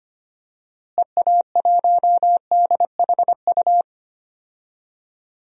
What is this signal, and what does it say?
Morse code, heard as a keyed tone, about 25 wpm, 700 Hz. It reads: EA1DHU
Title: DXCC-100 Call Signs - 6 Characters - Repeated 3x - 25wpm